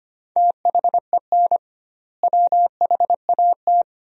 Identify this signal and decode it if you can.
Morse code, heard as a keyed tone, about 25 wpm, 700 Hz. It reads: THEN WHAT
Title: 2x Word-Spacing - Sentences from Top 400 Words - 25wpm